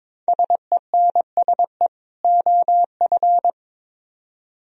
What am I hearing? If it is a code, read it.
Morse code, heard as a keyed tone, about 26 wpm, 700 Hz. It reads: SENSE OF
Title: Speed-Racing - Sets of 2 Words 22wpm